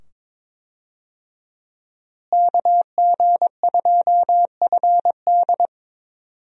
Morse code, heard as a keyed tone, about 22 wpm, 700 Hz. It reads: KG2FD
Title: USA Call Signs - 5 Characters - Repeated 2x - 22wpm